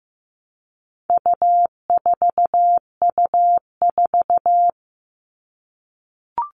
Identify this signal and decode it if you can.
Morse code, heard as a keyed tone, about 15 wpm, 700 Hz. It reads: U4U4
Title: All Cut Numbers - 4-Digit 15wpm